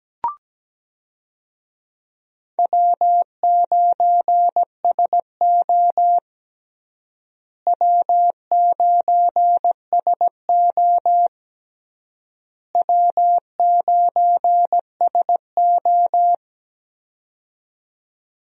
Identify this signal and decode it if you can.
Morse code, heard as a keyed tone, about 17 wpm, 700 Hz. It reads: W9SO W9SO W9SO
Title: USA Call Signs - 4 Characters - Repeated 3x - 17wpm